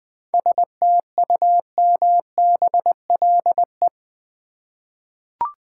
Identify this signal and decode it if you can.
Morse code, heard as a keyed tone, about 20 wpm, 700 Hz. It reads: STUMBLE
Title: 7 Letter Words - First 3 Letters Given - 20wpm